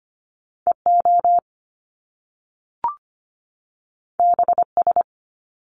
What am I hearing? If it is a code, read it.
Morse code, heard as a keyed tone, about 25 wpm, 700 Hz. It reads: EO BS
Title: Top Two Letter Combinations in Words 25wpm